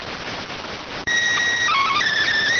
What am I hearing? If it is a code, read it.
SSTV (slow-scan television), PD 290.